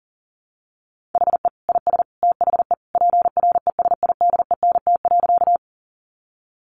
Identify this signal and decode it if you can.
Morse code, heard as a keyed tone, about 40 wpm, 700 Hz. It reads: HE IS THE PRESIDENT.
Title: Sentences from Top 1000 Words 40wpm